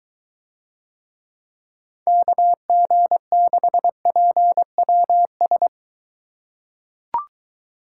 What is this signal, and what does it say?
Morse code, heard as a keyed tone, about 20 wpm, 700 Hz. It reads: KG6PWS